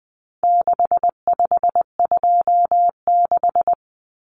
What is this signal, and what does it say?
Morse code, heard as a keyed tone, about 20 wpm, 700 Hz. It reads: 6526
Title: Numbers - 4-Digit 20wpm